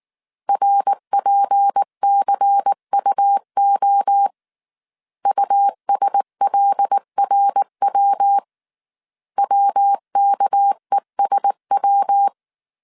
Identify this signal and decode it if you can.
Morse code, heard as a keyed tone, about 19 wpm, 800 Hz. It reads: RPCUO USLRW WKESW
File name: CW / DET / 1 6 1500C S C 18wpm 2356